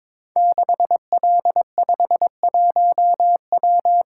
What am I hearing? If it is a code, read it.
Morse code, heard as a keyed tone, about 22 wpm, 700 Hz. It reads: 6L51W
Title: Code Groups - 5 Letter:Numbers - 22wpm